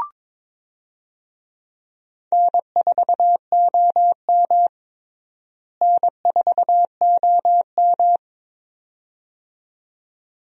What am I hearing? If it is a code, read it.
Morse code, heard as a keyed tone, about 22 wpm, 700 Hz. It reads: N4OM N4OM
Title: USA Call Signs - 4 Characters - Repeated 2x - 22wpm